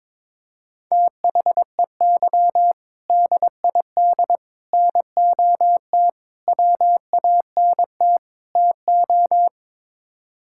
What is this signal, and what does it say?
Morse code, heard as a keyed tone, about 22 wpm, 700 Hz. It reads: THEY DID NOT WANT TO
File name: practice / 22wpm / Build A Phrase - 5-Word Phrases - 22wpm